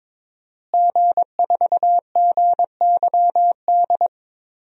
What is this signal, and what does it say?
Morse code, heard as a keyed tone, about 22 wpm, 700 Hz. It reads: G4GYD
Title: DXCC-100 Call Signs - 5 Characters - Repeated 2x - 22wpm